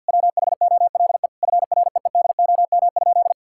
Morse code, heard as a keyed tone, about 50 wpm, 700 Hz. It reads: WHOLE FREEDOM?
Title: Sentences from Top 1000 Words - Review Part 5 - 50wpm